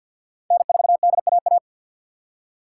Morse code, heard as a keyed tone, about 60 wpm, 700 Hz. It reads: N4DAA
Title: Speed Racing - USA Call Signs 50wpm